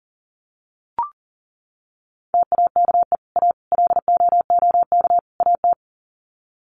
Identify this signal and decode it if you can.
Morse code, heard as a keyed tone, about 40 wpm, 700 Hz. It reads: TAKE A LOOK AT
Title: Sets of 4 Words 40wpm